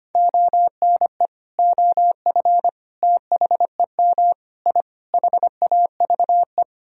Morse code, heard as a keyed tone, about 25 wpm, 700 Hz. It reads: ONE OF THEM I HAVE
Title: Sentences from Top 1000 Words - Review Part 1 - 25wpm